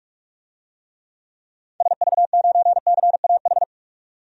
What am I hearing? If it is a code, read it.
Morse code, heard as a keyed tone, about 46 wpm, 700 Hz. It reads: SV9CAH
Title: DXCC-100 Call Signs - 6 Characters - 45wpm